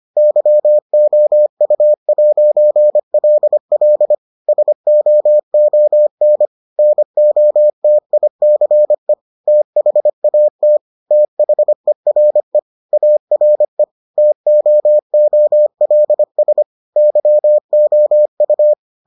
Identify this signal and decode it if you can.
Morse code, heard as a keyed tone, about 25 wpm, 600 Hz. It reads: YOU'LL SOON NOTICE THAT THERE ARE TOOLS YOU